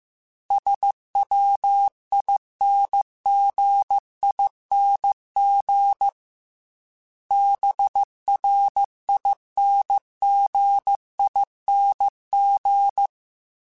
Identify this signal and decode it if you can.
Morse code, heard as a keyed tone, about 15 wpm, 800 Hz. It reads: SWINGING BRINGING